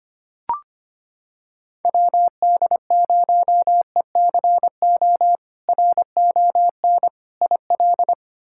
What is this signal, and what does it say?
Morse code, heard as a keyed tone, about 25 wpm, 700 Hz. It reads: WD0ECO RON IL
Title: K1USN SST QSO Practice - 25wpm